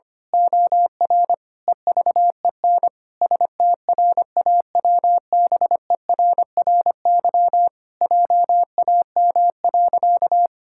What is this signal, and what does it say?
Morse code, heard as a keyed tone, about 25 wpm, 700 Hz. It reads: OR EVEN STRAWBERRY JAM.
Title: Pangram Challenge v2 - 25wpm